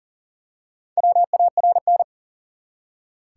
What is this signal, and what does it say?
Morse code, heard as a keyed tone, about 40 wpm, 700 Hz. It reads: WARN
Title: Four Letter Words - 40wpm